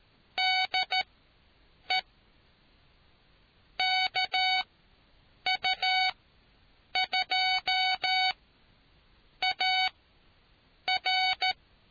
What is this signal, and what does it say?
Morse code, heard as a keyed tone, about 13 wpm, spaced as for 7 wpm, 750 Hz. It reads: DE KU2AR